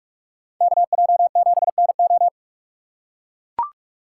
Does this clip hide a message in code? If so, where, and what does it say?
Yes: Morse code from 0.6 to 2.3 s, heard as a keyed tone, about 44 wpm, 700 Hz. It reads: KJ7NO